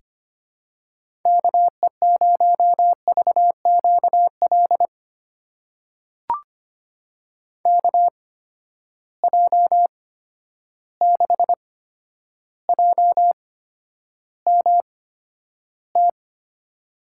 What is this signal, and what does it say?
Morse code, heard as a keyed tone, about 25 wpm, spaced as for 11 wpm, 700 Hz. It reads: KE0VQL KJ6JMT
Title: USA Call Signs - 6 Characters - 8x Character Spacing - 25wpm